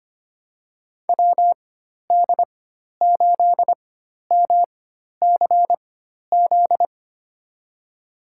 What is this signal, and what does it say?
Morse code, heard as a keyed tone, about 25 wpm, spaced as for 12 wpm, 700 Hz. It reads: WD8MCZ
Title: USA Call Signs - 6 Characters - 4x Character Spacing - 25wpm